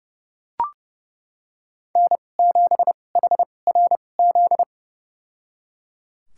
Morse code, heard as a keyed tone, about 30 wpm, spaced as for 22 wpm, 700 Hz. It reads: N7HRZ
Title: USA Call Signs - 5 Characters - 2x Character Spacing - 30wpm